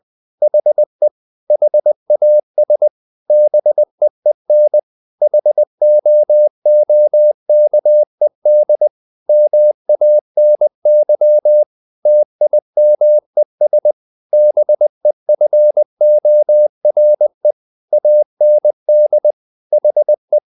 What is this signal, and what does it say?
Morse code, heard as a keyed tone, about 20 wpm, 600 Hz. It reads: HE HAS BEEN HOOKED MANY TIMES BEFORE AND HE
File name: the old man and the sea / the old man and the sea part 1 20wpm